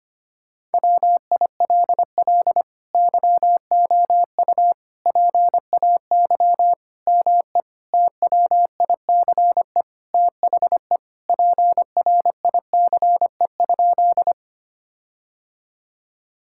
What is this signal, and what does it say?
Morse code, heard as a keyed tone, about 25 wpm, 700 Hz. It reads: WILL YOU PAY ME TWICE THE PRICE?
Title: Sentences from Top 1600 Words 25wpm